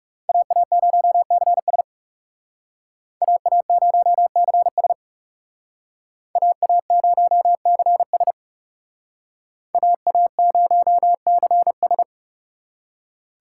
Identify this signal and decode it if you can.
Morse code, heard as a keyed tone, about 38 wpm, 700 Hz. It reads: AA0CS AA0CS AA0CS AA0CS